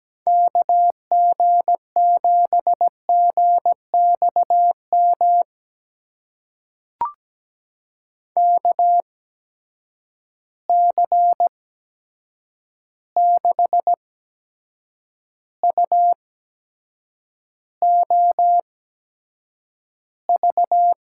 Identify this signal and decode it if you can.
Morse code, heard as a keyed tone, about 17 wpm, spaced as for 7 wpm, 700 Hz. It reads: KG7GXM KC6UOV
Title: USA Call Signs - 6 Characters - 8x Character Spacing - 17wpm